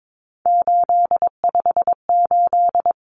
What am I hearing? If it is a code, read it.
Morse code, heard as a keyed tone, about 22 wpm, 700 Hz. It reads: 858
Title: Numbers - 3-Digit 22wpm